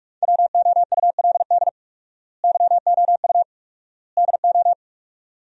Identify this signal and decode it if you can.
Morse code, heard as a keyed tone, about 41 wpm, 700 Hz. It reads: WOULD YOU DO
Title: 4x Word-Spacing - Sentences from Top 100 Words - 45wpm